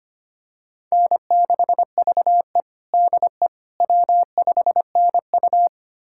Morse code, heard as a keyed tone, about 25 wpm, 700 Hz. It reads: N6VE DE W5NU
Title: USA Callsign DE Callsign - 4 Characters - 25wpm